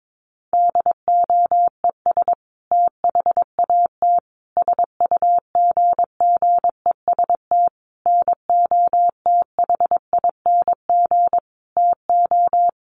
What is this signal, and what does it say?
Morse code, heard as a keyed tone, about 22 wpm, 700 Hz. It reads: DOES THAT SUGGEST NOTHING TO
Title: Sentences from Top 1300 Words 22wpm